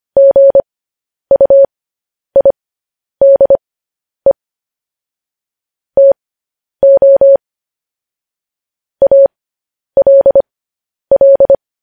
Morse code, heard as a keyed tone, about 25 wpm, spaced as for 10 wpm, 550 Hz. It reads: GUIDE TO ALL
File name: podcast / News headlines-1x-25x10